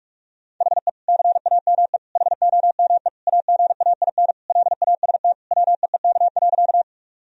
Morse code, heard as a keyed tone, about 41 wpm, 700 Hz. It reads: HE CAME HOME AGAIN LAST WEEK.
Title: Sentences from Top 1000 Words - Review Part 2 - 45wpm